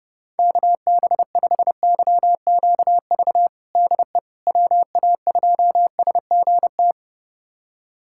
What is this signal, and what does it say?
Morse code, heard as a keyed tone, about 30 wpm, 700 Hz. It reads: KB5YQV DE WA2SGT
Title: USA Callsign DE Callsign 30wpm